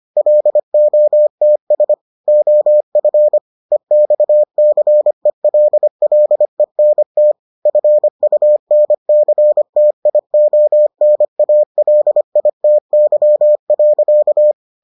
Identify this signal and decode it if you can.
Morse code, heard as a keyed tone, about 25 wpm, 600 Hz. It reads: LOTS OF EXCELLENT FUNCTIONALITY.